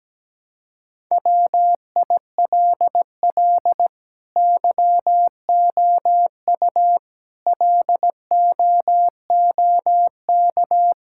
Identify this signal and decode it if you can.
Morse code, heard as a keyed tone, about 17 wpm, 700 Hz. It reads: WILL YOU LOOK
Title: Sentences from Top 1000 Words - Review Part 5 - 17wpm